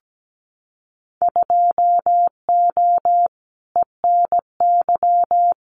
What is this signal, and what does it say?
Morse code, heard as a keyed tone, about 17 wpm, 700 Hz. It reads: 2O ENY
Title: Winter Field Day Exchanges 17wpm